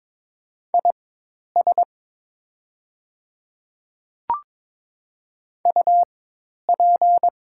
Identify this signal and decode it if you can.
Morse code, heard as a keyed tone, about 22 wpm, spaced as for 10 wpm, 700 Hz. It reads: IS UP
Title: 4x Character Spacing - Top 100 Words - 22wpm